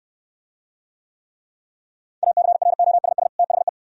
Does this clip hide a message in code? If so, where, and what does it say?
Yes: Morse code from 2.2 to 3.7 s, heard as a keyed tone, about 69 wpm, 700 Hz. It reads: ACROSS THE